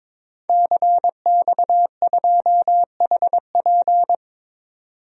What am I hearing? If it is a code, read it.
Morse code, heard as a keyed tone, about 22 wpm, 700 Hz. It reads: CX2HP
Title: DXCC-100 Call Signs - 5 Characters - 22wpm